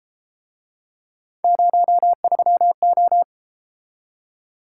Morse code, heard as a keyed tone, about 33 wpm, 700 Hz. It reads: 03O